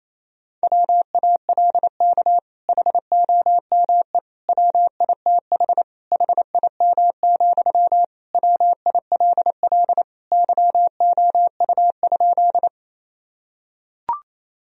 Morse code, heard as a keyed tone, about 27 wpm, 700 Hz. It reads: WALK HOME WITH HIM, WILL YOU?
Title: Sentences from Top 1000 Words - Review Part 1 - 28wpm